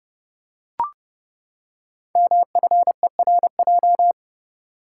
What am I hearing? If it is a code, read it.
Morse code, heard as a keyed tone, about 30 wpm, 700 Hz. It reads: MFERJ
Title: Code Groups - 5 Letters - 30wpm